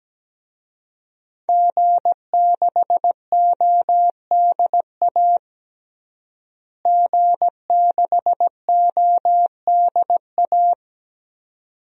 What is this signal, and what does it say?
Morse code, heard as a keyed tone, about 17 wpm, 700 Hz. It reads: G6ODA G6ODA
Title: DXCC-100 Call Signs - 5 Characters - Repeated 2x - 17wpm